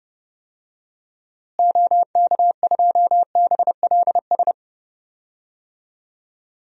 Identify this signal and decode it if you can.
Morse code, heard as a keyed tone, about 30 wpm, 700 Hz. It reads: OK2BLS
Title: DXCC-100 Call Signs - 6 Characters - Repeated 2x - 30wpm